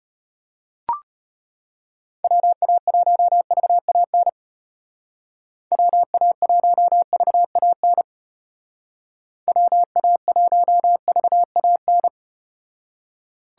Morse code, heard as a keyed tone, about 32 wpm, 700 Hz. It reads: WA1VAN WA1VAN WA1VAN